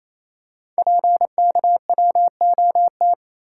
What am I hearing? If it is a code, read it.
Morse code, heard as a keyed tone, about 28 wpm, 700 Hz. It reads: PKWOT
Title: Code Groups - 5 Letters - 28wpm